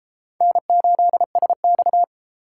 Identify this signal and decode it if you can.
Morse code, heard as a keyed tone, about 30 wpm, 700 Hz. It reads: N8SX